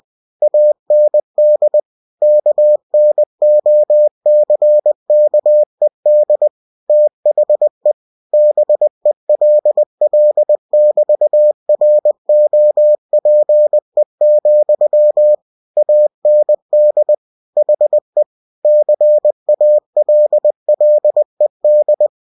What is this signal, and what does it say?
Morse code, heard as a keyed tone, about 20 wpm, 600 Hz. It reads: AND KNOCKED THE BELL<BT>ROPE, AND HE CALLED
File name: cw-books / winnie-the-pooh / winnie-20wpm-page-0029